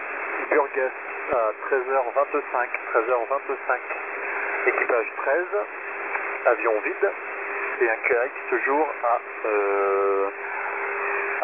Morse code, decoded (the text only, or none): none